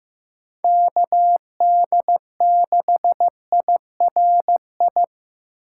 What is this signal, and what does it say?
Morse code, heard as a keyed tone, about 15 wpm, 700 Hz. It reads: KD6IRI
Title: K1USN SST QSO Practice - 15wpm